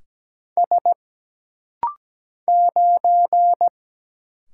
Morse code, heard as a keyed tone, about 17 wpm, 700 Hz. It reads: S 9
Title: Single Letter-Number - Rapid-Fire 17wpm